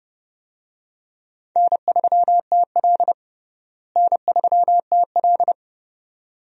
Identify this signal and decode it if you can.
Morse code, heard as a keyed tone, about 30 wpm, 700 Hz. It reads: N3TL N3TL
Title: USA Call Signs - 4 Characters - Repeated 2x - 30wpm